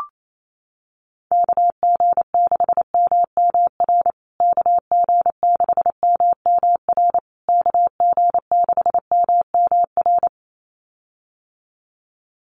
Morse code, heard as a keyed tone, about 28 wpm, 700 Hz. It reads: KG6MMR KG6MMR KG6MMR
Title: USA Call Signs Repeated 3x 28wpm